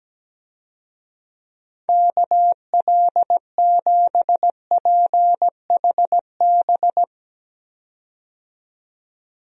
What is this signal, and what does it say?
Morse code, heard as a keyed tone, about 17 wpm, 700 Hz. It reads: KL7PHB
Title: DXCC-100 Call Signs - 6 Characters - 17wpm